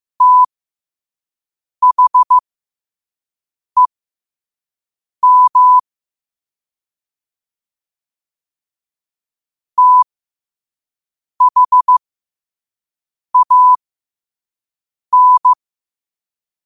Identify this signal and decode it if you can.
Morse code, heard as a keyed tone, about 15 wpm, spaced as for 5 wpm, 1000 Hz. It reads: THEM THAN
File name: qotdfiles / 5wpm